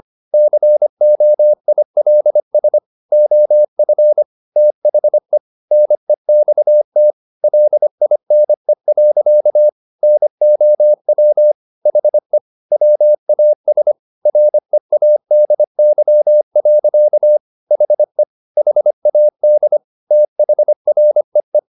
Morse code, heard as a keyed tone, about 25 wpm, 600 Hz. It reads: COILS OF THE NEXT LINE. NOW HE WAS READY. HE HAD THREE
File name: the old man and the sea part 1 25wpm